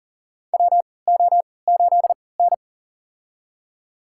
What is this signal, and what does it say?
Morse code, heard as a keyed tone, about 40 wpm, spaced as for 23 wpm, 700 Hz. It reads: WO8N
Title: USA Call Signs - 4 Characters - 3x Character Spacing - 40wpm